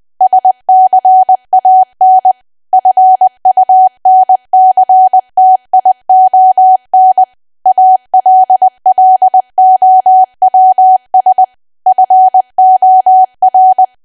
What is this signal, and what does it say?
Morse code, heard as a keyed tone, about 20 wpm, 750 Hz. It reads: SCAN FUNCTION ALLOWS FOR
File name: Archive / 20 WPM / 20WPM